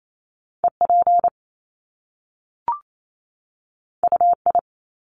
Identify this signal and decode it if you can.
Morse code, heard as a keyed tone, about 28 wpm, 700 Hz. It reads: EP UI
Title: Top Two Letter Combinations in Words 28wpm